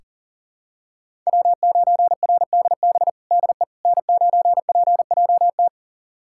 Morse code, heard as a keyed tone, about 40 wpm, 700 Hz. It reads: W9RDB DE N9PJT
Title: USA Callsign DE Callsign - 5 Characters - 40wpm